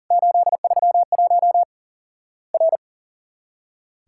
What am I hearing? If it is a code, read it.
Morse code, heard as a keyed tone, about 40 wpm, 700 Hz. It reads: 831 R
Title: Worked All Europe DX Contest - QTC Exchange - 40wpm